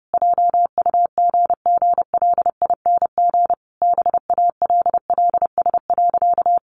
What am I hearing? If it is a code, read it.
Morse code, heard as a keyed tone, about 30 wpm, 700 Hz. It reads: JUGGLING BALLS.
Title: Easy Phrases 30wpm